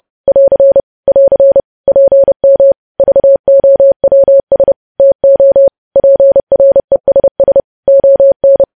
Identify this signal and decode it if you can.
Morse code, heard as a keyed tone, about 30 wpm, 550 Hz. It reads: <AR> <AR> PM VOWS TO PRESS ON
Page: News Headlines in Morse Code at 30 WPM